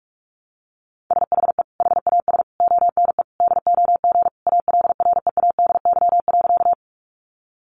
Morse code, heard as a keyed tone, about 44 wpm, 700 Hz. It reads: SHE HAS ONE DOG ALREADY.
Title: Easy Phrases 45wpm